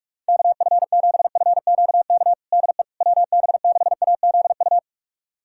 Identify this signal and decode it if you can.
Morse code, heard as a keyed tone, about 46 wpm, 700 Hz. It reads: KF7FQK DE WB6AZU
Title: USA Callsign DE Callsign - 6 Characters - 45wpm